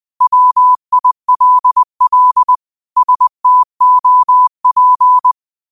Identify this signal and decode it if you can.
Morse code, heard as a keyed tone, about 20 wpm, 1000 Hz. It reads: WILL STOP